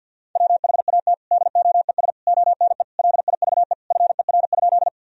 Morse code, heard as a keyed tone, about 50 wpm, 700 Hz. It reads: WHAT DOES ONE LIVE FER?
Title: Sentences from Top 200 Words and Abbreviations 50wpm